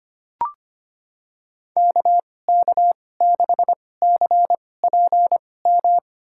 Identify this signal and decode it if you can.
Morse code, heard as a keyed tone, about 25 wpm, spaced as for 18 wpm, 700 Hz. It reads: KK6CPM
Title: USA Call Signs - 6 Characters - 2x Character Spacing - 25wpm